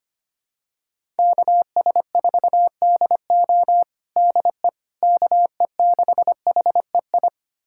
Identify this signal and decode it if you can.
Morse code, heard as a keyed tone, about 25 wpm, 700 Hz. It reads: KS4DO DE KE6HEI
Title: USA Callsign DE Callsign 25wpm